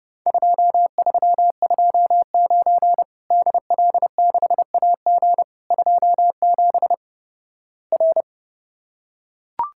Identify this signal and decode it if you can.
Morse code, heard as a keyed tone, about 30 wpm, 700 Hz. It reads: 2329 DL6AG 27 R
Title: Worked All Europe DX Contest - QTC Exchange - 30wpm